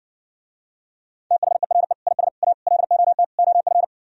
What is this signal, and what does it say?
Morse code, heard as a keyed tone, about 68 wpm, 700 Hz. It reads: THERE IS A LOT OF